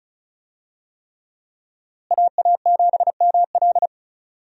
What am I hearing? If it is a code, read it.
Morse code, heard as a keyed tone, about 36 wpm, 700 Hz. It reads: AA7ML
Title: USA Call Signs - 5 Characters - Repeated 3x - 35wpm